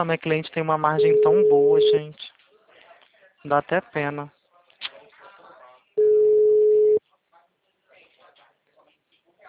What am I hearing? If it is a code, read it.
Ringback tone (Europe, 425 Hz), with speech.